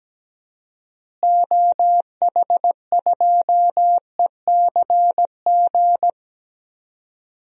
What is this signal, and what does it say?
Morse code, heard as a keyed tone, about 17 wpm, 700 Hz. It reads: OH2ECG
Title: DXCC-100 Call Signs - 6 Characters - Repeated 3x - 17wpm